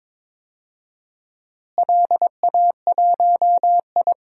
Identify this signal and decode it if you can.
Morse code, heard as a keyed tone, about 22 wpm, 700 Hz. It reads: LA1I
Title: DXCC-100 Call Signs - 4 Characters - Repeated 3x - 22wpm